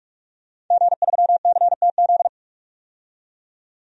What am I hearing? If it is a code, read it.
Morse code, heard as a keyed tone, about 43 wpm, 700 Hz. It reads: G3CTZ
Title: DXCC-100 Call Signs - 5 Characters - 45wpm